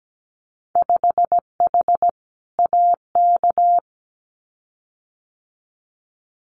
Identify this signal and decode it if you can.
Morse code, heard as a keyed tone, about 17 wpm, 700 Hz. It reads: 5H AK
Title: Winter Field Day Exchanges Encore 17wpm